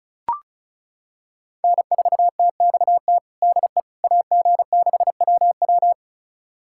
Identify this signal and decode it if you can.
Morse code, heard as a keyed tone, about 34 wpm, 690 Hz. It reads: N4TXT DE AG6WW
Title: USA Callsign DE Callsign - 5 Characters - 35wpm